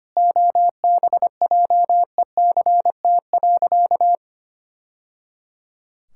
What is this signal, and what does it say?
Morse code, heard as a keyed tone, about 25 wpm, 700 Hz. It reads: OBJECT.
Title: Sentences from Top 1000 Words - Review Part 5 - 25wpm